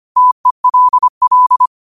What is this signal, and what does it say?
Morse code, heard as a keyed tone, about 25 wpm, 1000 Hz. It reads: TELL
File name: qotdfiles / 25wpm